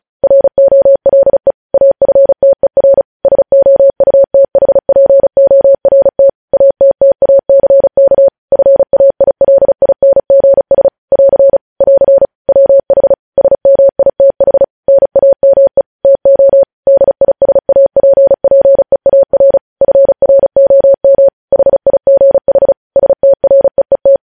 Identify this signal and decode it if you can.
Morse code, heard as a keyed tone, about 36 wpm, 550 Hz. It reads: ROLE AFTER SOUTHPORT ATTACK FAILINGS <AR> <AR> WH SMITH NAME TO DISAPPEAR FROM HIGH STREET
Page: News headlines in Morse code at 35 wpm